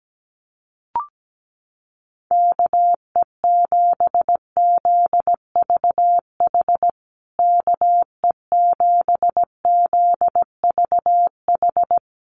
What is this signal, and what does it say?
Morse code, heard as a keyed tone, about 17 wpm, 700 Hz. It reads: KE7ZVH KE7ZVH
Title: USA Call Signs Repeated 3x 17wpm